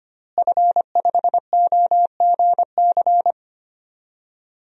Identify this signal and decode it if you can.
Morse code, heard as a keyed tone, about 25 wpm, 700 Hz. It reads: F5OGC
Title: DXCC-100 Call Signs - 5 Characters - Repeated 2x - 25wpm